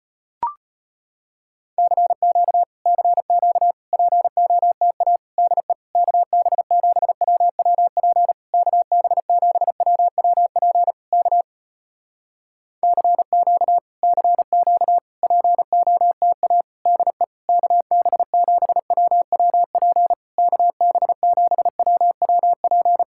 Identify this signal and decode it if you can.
Morse code, heard as a keyed tone, about 34 wpm, 700 Hz. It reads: CQ CQ POTA DE KB7WWP KB7WWP K CQ CQ POTA DE KB7WWP KB7WWP